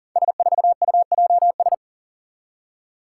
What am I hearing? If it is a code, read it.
Morse code, heard as a keyed tone, about 40 wpm, 690 Hz. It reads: S4UJS
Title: Five Challenging Letter-Number Combinations 40wpm